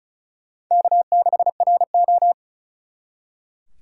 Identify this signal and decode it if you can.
Morse code, heard as a keyed tone, about 46 wpm, 700 Hz. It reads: K6RO